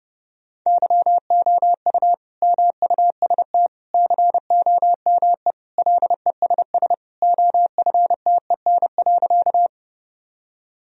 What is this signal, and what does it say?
Morse code, heard as a keyed tone, about 30 wpm, 700 Hz. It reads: YOU MUST COME LESS OFTEN.
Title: Sentences from Top 1000 Words - Review Part 2 - 30wpm